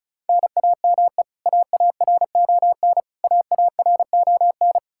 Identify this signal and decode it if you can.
Morse code, heard as a keyed tone, about 34 wpm, 700 Hz. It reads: NAME AARON AARON
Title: Practice QSO - Exchange 1 - Beginning - 35wpm